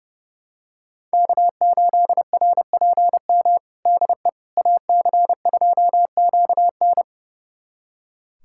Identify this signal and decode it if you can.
Morse code, heard as a keyed tone, about 32 wpm, 700 Hz. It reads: K8RPM DE AC2QN
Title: USA Callsign DE Callsign - 5 Characters - 30wpm